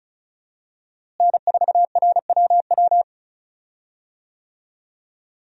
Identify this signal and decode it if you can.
Morse code, heard as a keyed tone, about 36 wpm, 700 Hz. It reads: N4RWW
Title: USA Call Signs - 5 Characters - Repeated 2x - 35wpm